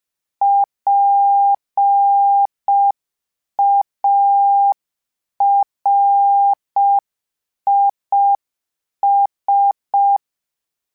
Morse code, heard as a keyed tone, about 5 wpm, 800 Hz. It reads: PARIS
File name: PARIS Calibration/PARIS mp3s/PARIS@5